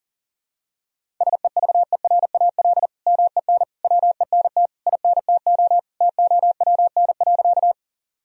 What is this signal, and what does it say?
Morse code, heard as a keyed tone, about 37 wpm, 700 Hz. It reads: SEVERAL MEN WENT INTO TOWN.
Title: Sentences from Top 1000 Words - Review Part 2 - 40wpm